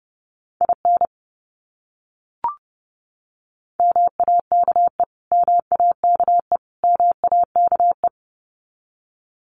Morse code, heard as a keyed tone, about 30 wpm, 700 Hz. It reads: IN MAKE MAKE MAKE
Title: Top 300 Words Repeated 30wpm